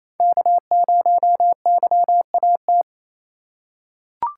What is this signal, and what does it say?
Morse code, heard as a keyed tone, about 28 wpm, 700 Hz. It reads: K0YAT
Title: USA Call Signs - 5 Characters - 2x Character Spacing - 28wpm